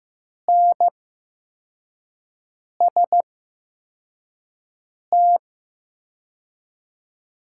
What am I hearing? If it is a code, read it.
Morse code, heard as a keyed tone, about 15 wpm, spaced as for 4 wpm, 700 Hz. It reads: NST
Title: 8x Character Spacing - Top 500 Words - 15wpm